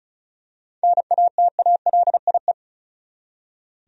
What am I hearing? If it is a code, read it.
Morse code, heard as a keyed tone, about 34 wpm, 700 Hz. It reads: NATALIE
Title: USA Common Names - 35wpm